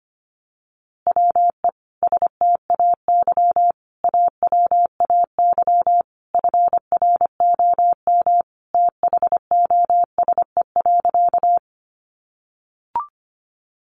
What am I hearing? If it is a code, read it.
Morse code, heard as a keyed tone, about 25 wpm, 700 Hz. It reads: WE STAY AWAY FROM THOSE.
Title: A Sound Of Thunder 25wpm